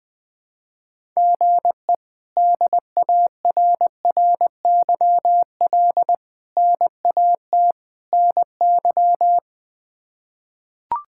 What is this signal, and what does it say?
Morse code, heard as a keyed tone, about 19 wpm, 700 Hz. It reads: GE DARRYL NAT NY
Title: K1USN SST QSO Practice - 20wpm